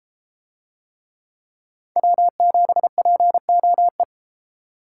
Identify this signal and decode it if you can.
Morse code, heard as a keyed tone, about 32 wpm, 700 Hz. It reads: W7POE